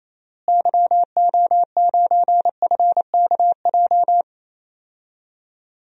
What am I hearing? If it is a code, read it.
Morse code, heard as a keyed tone, about 28 wpm, 700 Hz. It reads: YO9FKJ
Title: DXCC-100 Call Signs - 6 Characters - Repeated 2x - 28wpm